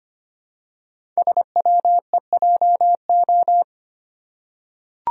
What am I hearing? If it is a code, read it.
Morse code, heard as a keyed tone, about 25 wpm, 700 Hz. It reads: SWEJO
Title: Code Groups - 5 Letters - 25wpm